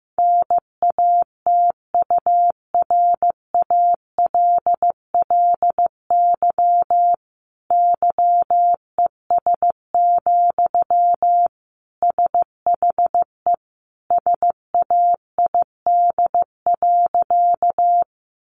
Morse code, heard as a keyed tone, about 15 wpm, 700 Hz. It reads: NATURALLY YES, SHE SAID.
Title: Sentences from Top 1300 Words 15wpm